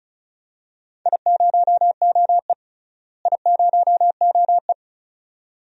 Morse code, heard as a keyed tone, about 36 wpm, 700 Hz. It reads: I0OE I0OE
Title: DXCC-100 Call Signs - 4 Characters - Repeated 2x - 35wpm